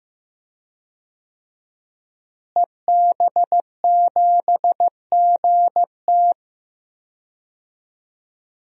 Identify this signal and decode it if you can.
Morse code, heard as a keyed tone, about 16 wpm, 700 Hz. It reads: EB7GT
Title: DXCC-100 Call Signs - 5 Characters - Repeated 2x - 15wpm